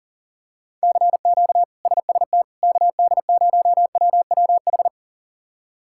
Morse code, heard as a keyed tone, about 37 wpm, 700 Hz. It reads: CQ SST KD0WWH
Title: K1USN SST QSO Practice - 40wpm